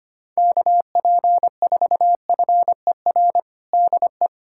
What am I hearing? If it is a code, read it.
Morse code, heard as a keyed tone, about 25 wpm, 700 Hz. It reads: KP4FER DE
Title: USA Callsign DE Callsign - 6 Characters - 25wpm